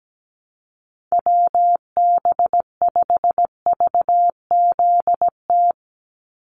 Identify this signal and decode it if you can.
Morse code, heard as a keyed tone, about 17 wpm, 700 Hz. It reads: WB5VZT
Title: USA Call Signs 17wpm